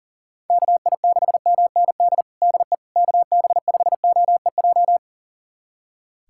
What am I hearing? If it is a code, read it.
Morse code, heard as a keyed tone, about 37 wpm, 700 Hz. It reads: KI6MND DE KB5OEJ
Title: USA Callsign DE Callsign 40wpm